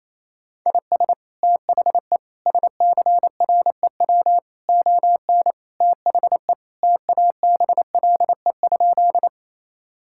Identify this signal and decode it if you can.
Morse code, heard as a keyed tone, about 28 wpm, 700 Hz. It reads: IS THE SCREW ON THE TABLE?
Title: Sentences from Top 1600 Words 28wpm